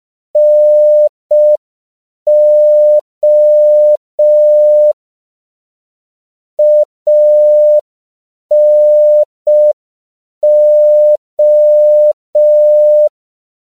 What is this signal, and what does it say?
Morse code, heard as a keyed tone, about 5 wpm, 600 Hz. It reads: NO ANO
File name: CW / 5PPM